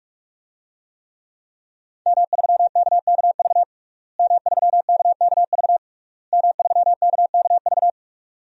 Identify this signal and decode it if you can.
Morse code, heard as a keyed tone, about 46 wpm, 700 Hz. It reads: M3KKV M3KKV M3KKV
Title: DXCC-100 Call Signs - 5 Characters - Repeated 3x - 45wpm